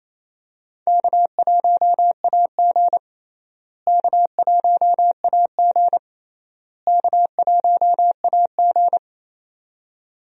Morse code, heard as a keyed tone, about 28 wpm, 700 Hz. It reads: K1AG K1AG K1AG
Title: USA Call Signs - 4 Characters - Repeated 3x - 28wpm